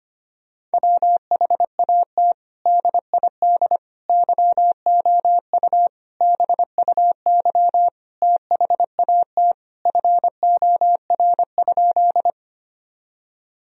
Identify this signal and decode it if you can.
Morse code, heard as a keyed tone, about 25 wpm, 700 Hz. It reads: WHAT DID YOU BUY THAT FOR?
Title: Sentences from Top 1000 Words - Review Part 4 - 25wpm